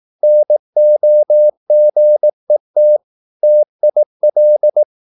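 Morse code, heard as a keyed tone, about 18 wpm, 600 Hz. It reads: NOGET TIL
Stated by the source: Slow (QRS) Morse practice at 18 wpm, CW med Gnister.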